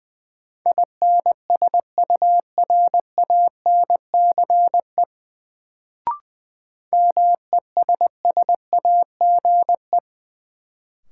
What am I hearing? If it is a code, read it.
Morse code, heard as a keyed tone, about 20 wpm, 700 Hz. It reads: INSURANCE MESSAGE
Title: Sentences from Top 1100 Words - New Word Drill Alt - 20wpm